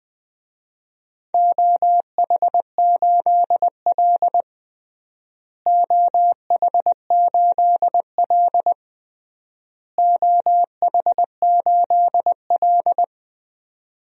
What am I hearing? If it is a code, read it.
Morse code, heard as a keyed tone, about 19 wpm, 700 Hz. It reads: OH8L OH8L OH8L
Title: DXCC-100 Call Signs - 4 Characters - Repeated 3x - 20wpm